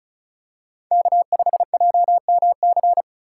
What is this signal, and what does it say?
Morse code, heard as a keyed tone, about 36 wpm, 700 Hz. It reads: K5JMC
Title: USA Call Signs - 2x Character Spacing - 35wpm